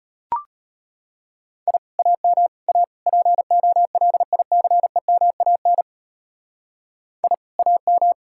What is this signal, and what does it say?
Morse code, heard as a keyed tone, about 30 wpm, 700 Hz. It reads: I AM A POLICEMAN I AM
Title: Speed-Racing Common QSO Phrases - 25wpm